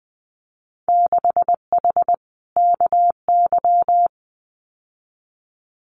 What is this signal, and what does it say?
Morse code, heard as a keyed tone, about 20 wpm, 700 Hz. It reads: 6H KY
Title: Winter Field Day Exchanges Encore 20wpm